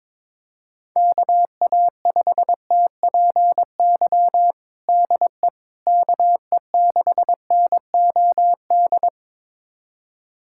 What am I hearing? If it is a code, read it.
Morse code, heard as a keyed tone, about 22 wpm, 700 Hz. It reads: KA5TPY DE KE6NOD
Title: USA Callsign DE Callsign - 6 Characters - 22wpm